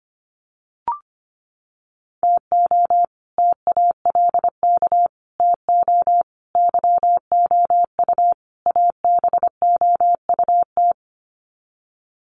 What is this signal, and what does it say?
Morse code, heard as a keyed tone, about 25 wpm, 700 Hz. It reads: TO TALK TO YOU ABOUT
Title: Sets of 5 Words Encore 25wpm